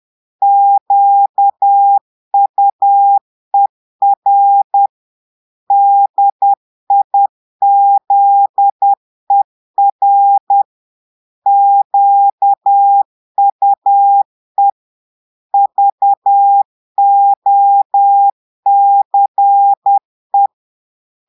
Morse code, heard as a keyed tone, about 10 wpm, 800 Hz. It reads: QUER DIZER QUE VOCE
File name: portugais texte en clair 2 vitesse 10 mots minutes